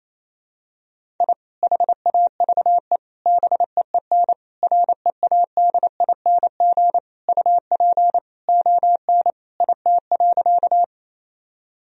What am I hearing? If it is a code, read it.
Morse code, heard as a keyed tone, about 28 wpm, 700 Hz. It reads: I HAVE BEEN READING UP ON IT.
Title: Sentences from Top 1000 Words - Review Part 3 - 28wpm